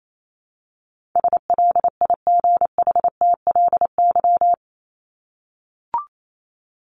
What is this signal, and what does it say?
Morse code, heard as a keyed tone, about 28 wpm, 700 Hz. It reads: SLIGHTLY
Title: LY Suffix - Top 100 Words with Base Words 28wpm